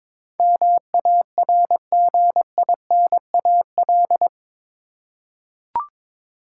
Morse code, heard as a keyed tone, about 22 wpm, 700 Hz. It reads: MARGINAL
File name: practice / 22wpm / Eight Letter Words - 22wpm